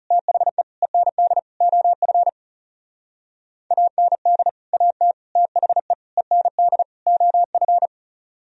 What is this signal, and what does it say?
Morse code, heard as a keyed tone, about 36 wpm, 700 Hz. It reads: THE END OF AND AT THE END OF
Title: Speed-Racing - Sets of 5 Words 30wpm